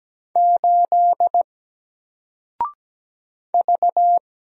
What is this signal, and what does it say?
Morse code, heard as a keyed tone, about 17 wpm, 700 Hz. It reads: 8 V
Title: Single Letter-Number 17wpm